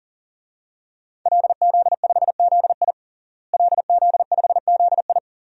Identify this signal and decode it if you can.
Morse code, heard as a keyed tone, about 37 wpm, 700 Hz. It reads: LZ5ZI LZ5ZI
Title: DXCC-100 Call Signs - 5 Characters - Repeated 2x - 40wpm